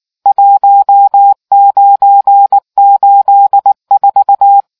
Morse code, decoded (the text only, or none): 1984